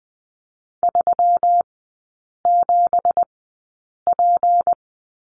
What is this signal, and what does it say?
Morse code, heard as a keyed tone, about 20 wpm, spaced as for 8 wpm, 700 Hz. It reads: 37P
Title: 166 Lesson 27e - Number 7 - Three Characters 20wpm